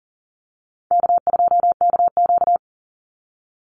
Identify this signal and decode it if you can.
Morse code, heard as a keyed tone, about 40 wpm, 700 Hz. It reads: K2KQ
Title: Easy Call Signs - 40wpm